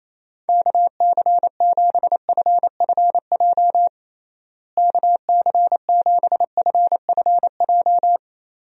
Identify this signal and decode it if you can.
Morse code, heard as a keyed tone, about 27 wpm, 700 Hz. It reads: KC7FFJ KC7FFJ
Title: USA Call Signs - 6 Characters - Repeated 2x - 28wpm